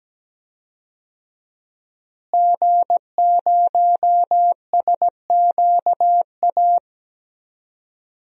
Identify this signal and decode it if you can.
Morse code, heard as a keyed tone, about 17 wpm, 700 Hz. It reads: G0SQA
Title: DXCC-100 Call Signs - 5 Characters - Repeated 2x - 17wpm